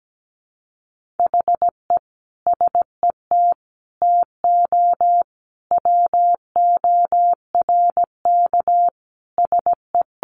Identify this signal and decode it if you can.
Morse code, heard as a keyed tone, about 17 wpm, 700 Hz. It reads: HE SET TO WORK SE
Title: Sentences from Top 1200 Words 17wpm